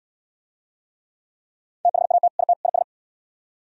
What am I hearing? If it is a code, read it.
Morse code, heard as a keyed tone, about 75 wpm, 700 Hz. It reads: THAT IT IS